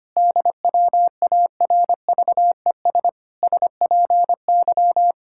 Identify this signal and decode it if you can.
Morse code, heard as a keyed tone, about 25 wpm, 700 Hz. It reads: DWARVES SPY
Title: Pangram Challenge v2 - 25wpm